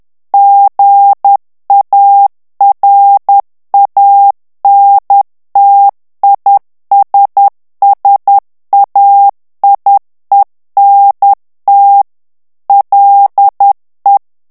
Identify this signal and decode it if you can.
Morse code, heard as a keyed tone, about 10 wpm, 800 Hz. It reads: GARANTISSAIENT LE
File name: morse mp3 / FRANCAIS TEXTE CLAIR 2 VITESSE 10 MOTS MINUTES